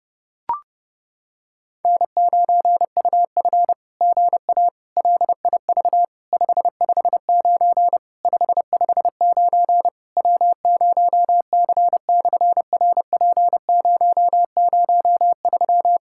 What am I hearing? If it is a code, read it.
Morse code, heard as a keyed tone, about 30 wpm, 700 Hz. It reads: N9UF GA LIV 559 559 W0C/RP003
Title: SOTA QSO Practice 30wpm